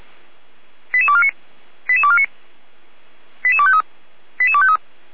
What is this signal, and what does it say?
Five-tone selective call (ZVEI), heard as quick runs of five tones, tones of about 70 ms. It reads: 88249 88249 88253 88253
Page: SDR/ZVEI